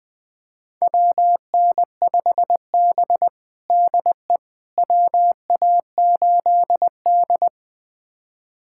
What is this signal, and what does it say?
Morse code, heard as a keyed tone, about 20 wpm, 700 Hz. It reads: WN5B DE WA8D
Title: USA Callsign DE Callsign - 4 Characters - 20wpm